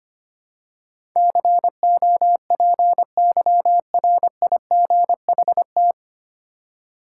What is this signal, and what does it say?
Morse code, heard as a keyed tone, about 25 wpm, 700 Hz. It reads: COPYRIGHT
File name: Nine Letter Words - 25wpm